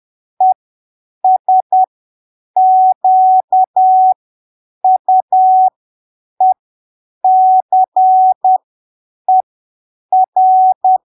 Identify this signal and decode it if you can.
Morse code, heard as a keyed tone, about 10 wpm, spaced as for 7 wpm, 750 Hz. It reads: ESQUECER
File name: arquivos / texto / texto 21 10ppm